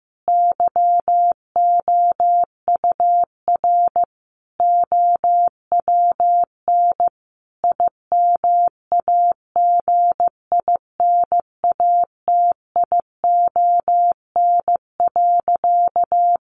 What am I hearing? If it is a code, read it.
Morse code, heard as a keyed tone, about 15 wpm, 700 Hz. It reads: YOUR OWN IMAGINATION.
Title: Sentences from Top 1300 Words 15wpm